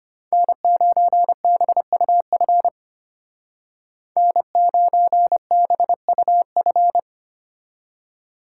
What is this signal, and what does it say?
Morse code, heard as a keyed tone, about 31 wpm, 700 Hz. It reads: N9BUF N9BUF
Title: Speed Racing - USA 1x3 Call Signs 25wpm